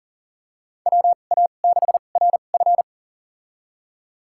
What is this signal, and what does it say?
Morse code, heard as a keyed tone, about 40 wpm, spaced as for 29 wpm, 700 Hz. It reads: WA6RF
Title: USA Call Signs - 2x Character Spacing - 40wpm